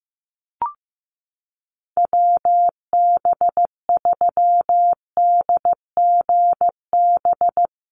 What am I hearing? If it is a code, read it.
Morse code, heard as a keyed tone, about 15 wpm, 700 Hz. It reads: WB3DGB
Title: USA Call Signs Repeated 3x 15wpm